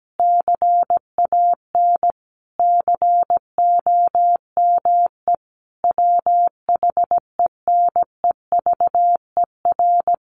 Morse code, heard as a keyed tone, about 17 wpm, 700 Hz. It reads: CAN COME WHENEVER
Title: Sentences from Top 1200 Words 17wpm